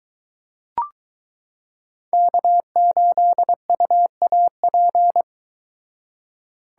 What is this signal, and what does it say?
Morse code, heard as a keyed tone, about 22 wpm, 700 Hz. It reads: K8UAP